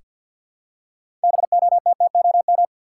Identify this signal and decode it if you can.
Morse code, heard as a keyed tone, about 49 wpm, 700 Hz. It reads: BOTTOM
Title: Six Letter Words - 50wpm